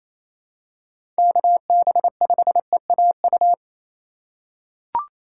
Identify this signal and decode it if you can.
Morse code, heard as a keyed tone, about 28 wpm, 700 Hz. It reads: KB5EAU
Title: USA Call Signs - 6 Characters - 8x Character Spacing - 28wpm